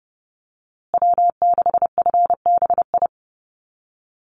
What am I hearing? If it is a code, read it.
Morse code, heard as a keyed tone, about 30 wpm, 700 Hz. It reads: W6FBI
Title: USA Call Signs Repeated 2x 30wpm